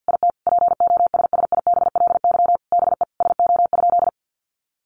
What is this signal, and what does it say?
Morse code, heard as a keyed tone, about 50 wpm, 700 Hz. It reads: IT POSSIBLY BE SO?
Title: Sentences from Top 1300 Words 50wpm